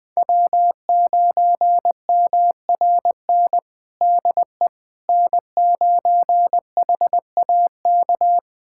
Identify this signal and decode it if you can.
Morse code, heard as a keyed tone, about 21 wpm, 700 Hz. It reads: W9MRN DE N9HAK
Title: USA Callsign DE Callsign - 5 Characters - 20wpm